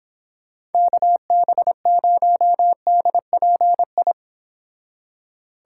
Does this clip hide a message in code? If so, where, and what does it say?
Yes: Morse code from 0.7 to 4.1 s, heard as a keyed tone, about 29 wpm, 700 Hz. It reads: KB0DPI